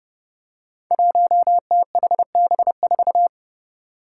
Morse code, heard as a keyed tone, about 32 wpm, 700 Hz. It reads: 1THB4
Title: Five Challenging Letter-Number Combinations 30wpm